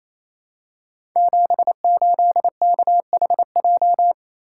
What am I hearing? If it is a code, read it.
Morse code, heard as a keyed tone, about 27 wpm, 700 Hz. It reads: 78KHJ
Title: Code Groups - 5 Letter:Numbers - 28wpm